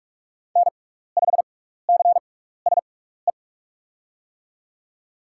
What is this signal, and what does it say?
Morse code, heard as a keyed tone, about 44 wpm, spaced as for 15 wpm, 700 Hz. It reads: N5CSE
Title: USA Call Signs - 5 Characters - 6x Character Spacing - 45wpm